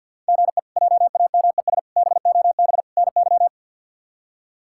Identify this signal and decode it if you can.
Morse code, heard as a keyed tone, about 52 wpm, 700 Hz. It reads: GE JAMES BOB NY
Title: K1USN SST QSO Practice - 50wpm